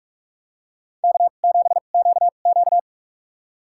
Morse code, heard as a keyed tone, about 46 wpm, spaced as for 33 wpm, 700 Hz. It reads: K7QQ
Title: USA Call Signs - 4 Characters - 2x Character Spacing - 45wpm